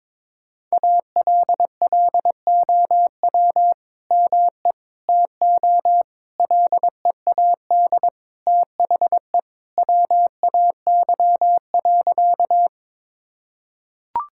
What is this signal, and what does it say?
Morse code, heard as a keyed tone, about 21 wpm, 700 Hz. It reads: ALLOW ME TO LEAD THE WAY.
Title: Sentences from Top 1000 Words - Review Part 5 - 22wpm